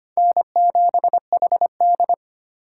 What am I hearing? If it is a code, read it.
Morse code, heard as a keyed tone, about 32 wpm, 700 Hz. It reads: N7HD